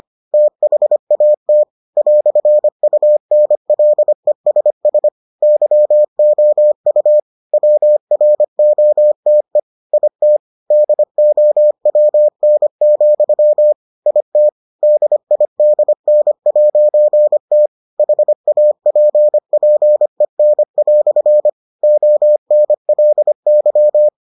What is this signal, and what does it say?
Morse code, heard as a keyed tone, about 25 wpm, 600 Hz. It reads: THAT "UNLESS YOU WROTE IT DOWN, IT DIDN'T HAPPEN" ONLY